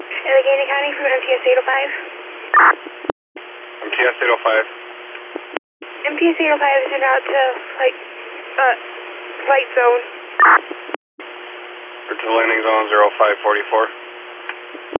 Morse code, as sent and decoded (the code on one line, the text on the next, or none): none